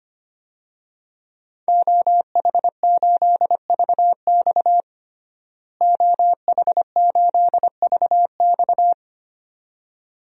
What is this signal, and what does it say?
Morse code, heard as a keyed tone, about 25 wpm, 700 Hz. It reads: OH8VX OH8VX
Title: DXCC-100 Call Signs - 5 Characters - Repeated 2x - 25wpm